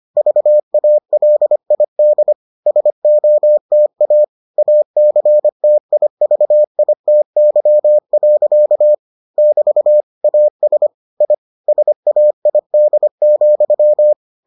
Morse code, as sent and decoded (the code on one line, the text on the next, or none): ...- .- .-.. .. -.. / ... --- - .- / .- -.-. - .. ...- .. - -.-- .-.-.- / -...- / .- ... / .. / ... .- .. -.. --..--
VALID SOTA ACTIVITY. <BT> AS I SAID,